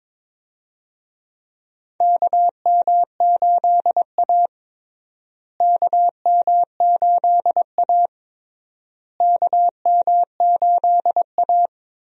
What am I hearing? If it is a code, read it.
Morse code, heard as a keyed tone, about 22 wpm, 700 Hz. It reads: KM8A KM8A KM8A
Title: USA Call Signs - 4 Characters - Repeated 3x - 22wpm